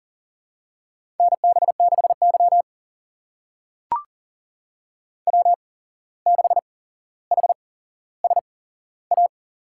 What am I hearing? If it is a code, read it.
Morse code, heard as a keyed tone, about 40 wpm, spaced as for 16 wpm, 700 Hz. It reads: NB6Y W6HSA
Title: USA Call Signs - 8x Character Spacing - 40wpm